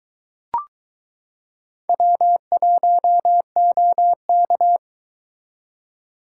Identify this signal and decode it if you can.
Morse code, heard as a keyed tone, about 22 wpm, 700 Hz. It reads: W1OK